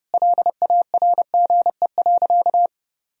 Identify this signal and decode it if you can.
Morse code, heard as a keyed tone, about 32 wpm, 700 Hz. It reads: LARGE.
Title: Sentences from Top 1000 Words - Review Part 1 - 30wpm